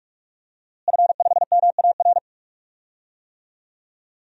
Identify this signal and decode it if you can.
Morse code, heard as a keyed tone, about 46 wpm, 700 Hz. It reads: F5MAR
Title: DXCC-100 Call Signs - 5 Characters - Repeated 2x - 45wpm